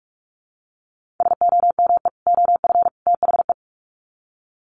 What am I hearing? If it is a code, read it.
Morse code, heard as a keyed tone, about 44 wpm, 700 Hz. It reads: SOME OF THE
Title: Sets of 3 Words 45wpm